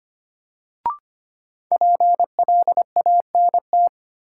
Morse code, heard as a keyed tone, about 25 wpm, 700 Hz. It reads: PLANT
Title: Sentences from Top 1000 Words - Review Part 1 - New Word Drill - 25wpm